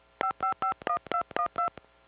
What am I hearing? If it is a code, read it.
DTMF, heard as telephone touch tones, tones of about 90 ms. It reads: 5551212